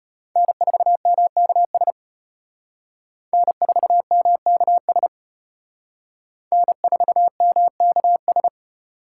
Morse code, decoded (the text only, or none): N4MKS N4MKS N4MKS